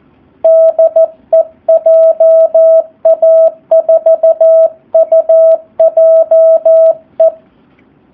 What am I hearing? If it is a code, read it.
Morse code, heard as a keyed tone, about 14 wpm, 650 Hz. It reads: DE JA4UJE
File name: jh8cbh / 2019cw / SP04- w